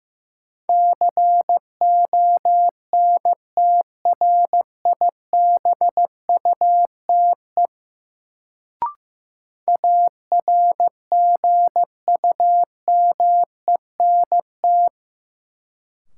Morse code, heard as a keyed tone, about 15 wpm, 700 Hz. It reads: CONTRIBUTE ARGUMENT
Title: Sentences from Top 1400 Words - New Word Drill - 15wpm